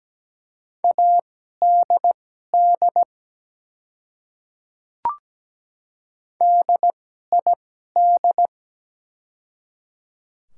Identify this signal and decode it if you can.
Morse code, heard as a keyed tone, about 17 wpm, spaced as for 12 wpm, 700 Hz. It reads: ADD DID
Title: 2x Character Spacing - Top 200 Words - 17wpm